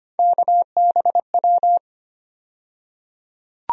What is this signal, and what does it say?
Morse code, heard as a keyed tone, about 25 wpm, 700 Hz. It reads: KBW
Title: USA Call Sign Postfixes - 3 Letters 25wpm